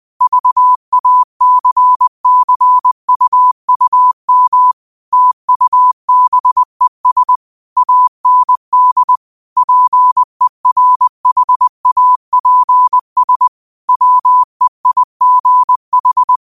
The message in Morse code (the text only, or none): VACCUUM TUBES AND PERHAPS WEIGH